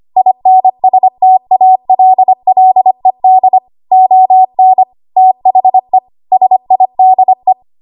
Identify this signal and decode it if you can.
Morse code, heard as a keyed tone, about 25 wpm, 750 Hz. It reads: INSTALLED ON THE SIDE
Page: 25 WPM Code Archive